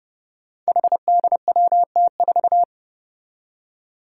Morse code, heard as a keyed tone, about 30 wpm, 700 Hz. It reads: HDWT4
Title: Five Challenging Letter-Number Combinations 30wpm